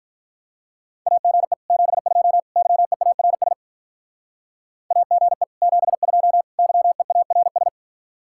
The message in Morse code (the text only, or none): AGE 72 YEARS AGE 72 YEARS